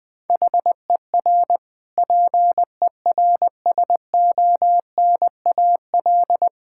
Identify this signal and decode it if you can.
Morse code, heard as a keyed tone, about 20 wpm, 700 Hz. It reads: HER PERSONAL
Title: Sentences from Top 1000 Words - Review Part 6 - 20wpm